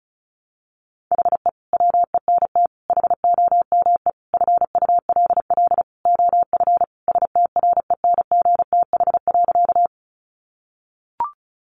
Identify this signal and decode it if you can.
Morse code, heard as a keyed tone, about 34 wpm, 700 Hz. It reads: HE WENT HOME FULL OF STRENGTH.
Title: Sentences from Top 1300 Words 35wpm